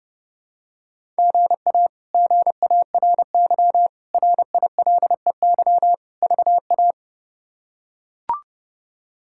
Morse code, heard as a keyed tone, about 31 wpm, 700 Hz. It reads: GA GARY RILEY VA
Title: K1USN SST QSO Practice - 30wpm